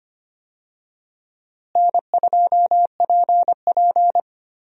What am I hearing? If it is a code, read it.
Morse code, heard as a keyed tone, about 25 wpm, 700 Hz. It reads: N2PP
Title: USA Call Signs - 4 Characters - Repeated 2x - 25wpm